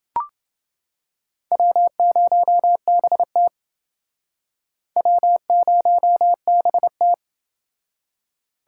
Morse code, heard as a keyed tone, about 30 wpm, 700 Hz. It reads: W0BT W0BT